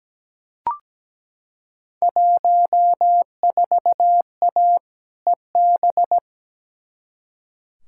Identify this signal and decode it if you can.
Morse code, heard as a keyed tone, about 17 wpm, 700 Hz. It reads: 14A EB
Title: Field Day Exchanges Encore - 17wpm